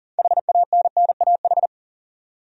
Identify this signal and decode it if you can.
Morse code, heard as a keyed tone, about 40 wpm, 700 Hz. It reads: HANNAH
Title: USA Common Names - 40wpm